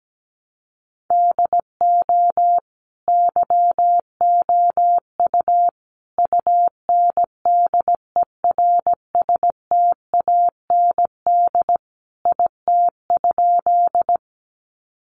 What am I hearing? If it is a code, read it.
Morse code, heard as a keyed tone, about 17 wpm, 700 Hz. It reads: DO YOU UNDERSTAND IT?
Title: Sentences from Top 600 Words 17wpm